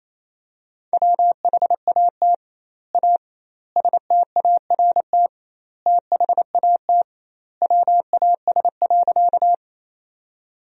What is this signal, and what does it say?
Morse code, heard as a keyed tone, about 27 wpm, 700 Hz. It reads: WHAT A START THAT WAS.
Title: 2x Word-Spacing - Sentences from Top 300 Words - 28wpm